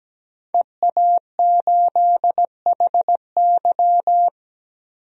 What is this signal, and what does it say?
Morse code, heard as a keyed tone, about 17 wpm, 700 Hz. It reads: EA8HY
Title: DXCC-100 Call Signs - 5 Characters - 17wpm